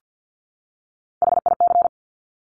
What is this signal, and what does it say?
Morse code, heard as a keyed tone, about 50 wpm, 700 Hz. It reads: HIC
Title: Top Three Letter Combinations in Words 50wpm